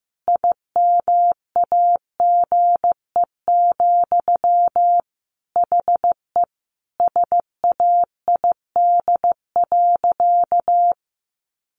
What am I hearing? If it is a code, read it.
Morse code, heard as a keyed tone, about 15 wpm, 700 Hz. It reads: IMAGE, HE SAID.
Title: Sentences from Top 700 Words 15wpm